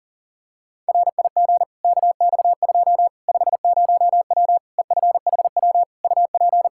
Morse code, heard as a keyed tone, about 40 wpm, 700 Hz. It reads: RIG KX2 50W EFHW UP
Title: K1USN SST QSO Practice 40wpm